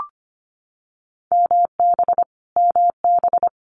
Morse code, heard as a keyed tone, about 25 wpm, 700 Hz. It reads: MB MB
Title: Multi-letter Phonemes Repeated 3x 25wpm